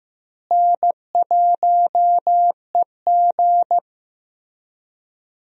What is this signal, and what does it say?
Morse code, heard as a keyed tone, about 15 wpm, 700 Hz. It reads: N1EG
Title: USA Call Signs - 4 Characters - Repeated 3x - 15wpm